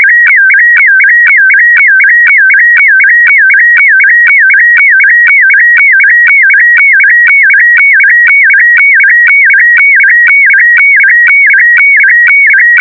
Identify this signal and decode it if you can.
Weather fax (radiofax).